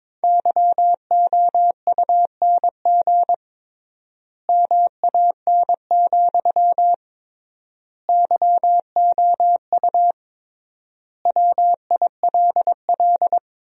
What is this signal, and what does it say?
Morse code, heard as a keyed tone, about 22 wpm, 700 Hz. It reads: YOUNG MAN, YOU WILL
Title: 3x Word-Spacing - Sentences from Top 400 Words - 22wpm